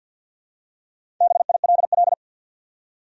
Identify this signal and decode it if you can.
Morse code, heard as a keyed tone, about 52 wpm, 700 Hz. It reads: BILL
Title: USA Common Names - Max 5 Letters - 50wpm